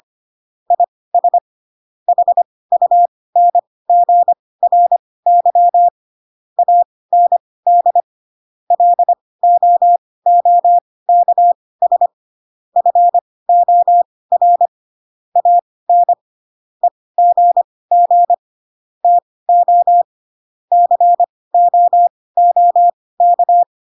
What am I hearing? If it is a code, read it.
Morse code, heard as a keyed tone, about 25 wpm, spaced as for 18 wpm, 700 Hz. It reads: IS HUNGRY AND LOOKS FOR AN EGG TO COOK
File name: Pinocchio / Pinocchio ch5 12 25 ik1hge